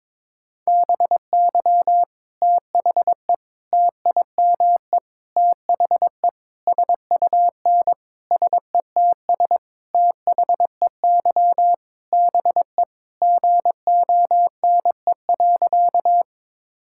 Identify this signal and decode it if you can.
Morse code, heard as a keyed tone, about 22 wpm, 700 Hz. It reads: BY THE TIME THE SUN SETS THEY BE GONE.
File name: practice / 22wpm / Sentences with 4-Word Phrases - 22wpm